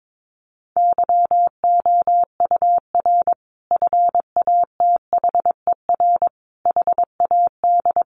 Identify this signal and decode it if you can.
Morse code, heard as a keyed tone, about 22 wpm, 700 Hz. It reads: YOUR FATHER HAD
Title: Sentences from Top 1100 Words 22wpm